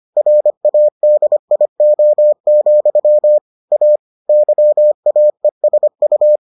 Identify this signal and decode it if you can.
Morse code, heard as a keyed tone, about 25 wpm, 600 Hz. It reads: RADIO, A YAESU